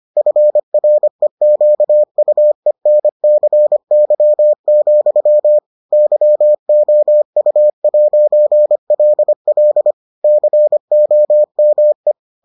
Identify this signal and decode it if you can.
Morse code, heard as a keyed tone, about 25 wpm, 600 Hz. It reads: FREQUENCY, YOU'LL COME